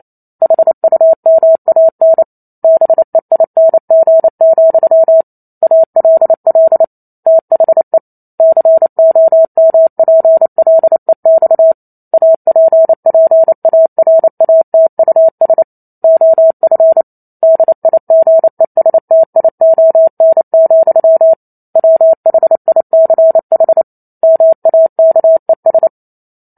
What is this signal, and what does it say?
Morse code, heard as a keyed tone, about 28 wpm, 650 Hz. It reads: HUMAN BEING, ALL THE COMPLEX APPARATUS OF DIGESTION, WHICH MAKES